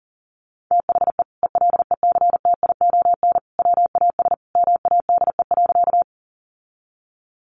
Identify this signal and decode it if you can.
Morse code, heard as a keyed tone, about 40 wpm, 700 Hz. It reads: THE ELECTION WAS MADE.
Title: Sentences from Top 1500 Words 40wpm